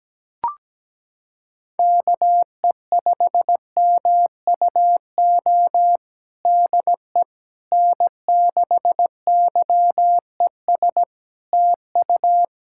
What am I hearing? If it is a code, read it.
Morse code, heard as a keyed tone, about 17 wpm, 700 Hz. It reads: KE5MUO DE N6YES TU
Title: K1USN SST QSO Practice 17wpm